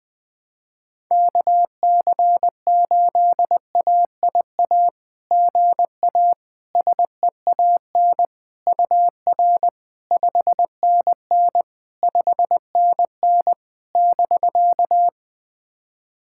Morse code, heard as a keyed tone, about 20 wpm, 700 Hz. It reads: KC8AIA GA SEAN UR 5NN 5NN <BK>
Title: SOTA QSO Practice 20wpm